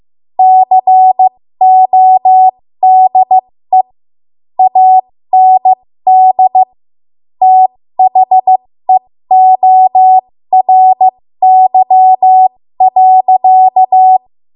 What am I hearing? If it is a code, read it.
Morse code, heard as a keyed tone, about 15 wpm, spaced as for 13 wpm, 750 Hz. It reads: CODE AND THEORY.
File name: Morse / 13WPM